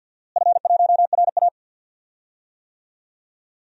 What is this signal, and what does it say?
Morse code, heard as a keyed tone, about 52 wpm, 700 Hz. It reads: F1RA
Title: DXCC-100 Call Signs - 4 Characters - Repeated 3x - 50wpm